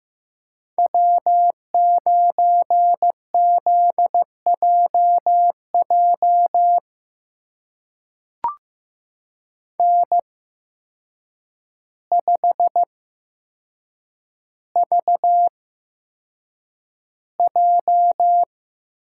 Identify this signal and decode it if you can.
Morse code, heard as a keyed tone, about 15 wpm, spaced as for 7 wpm, 700 Hz. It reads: W9ZJJ N5VJ
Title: USA Call Signs - 5 Characters - 8x Character Spacing - 15wpm